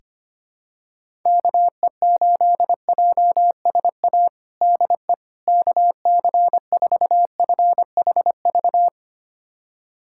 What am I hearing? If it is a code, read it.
Morse code, heard as a keyed tone, about 25 wpm, 700 Hz. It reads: KE8JSA DE KC4FHV
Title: USA Callsign DE Callsign - 6 Characters - 25wpm